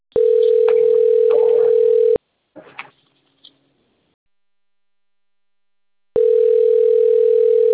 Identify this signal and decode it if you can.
Ringback tone (North America), 440 + 480 Hz.